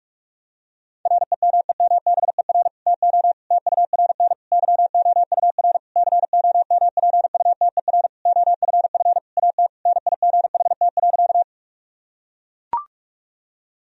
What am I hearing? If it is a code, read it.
Morse code, heard as a keyed tone, about 42 wpm, 700 Hz. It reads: REMEMBER TO TURN YOUR COMPUTER OFF AT NIGHT.
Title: Sentences from Top 1000 Words - Review Part 3 - 45wpm